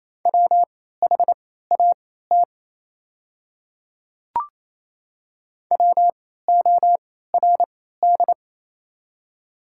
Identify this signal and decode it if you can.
Morse code, heard as a keyed tone, about 28 wpm, spaced as for 16 wpm, 700 Hz. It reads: WHAT WORD
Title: 3x Character Spacing - Top 100 Words - 28wpm